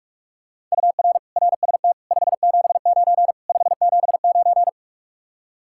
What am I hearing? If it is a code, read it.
Morse code, heard as a keyed tone, about 50 wpm, 700 Hz. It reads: UR RST 579 579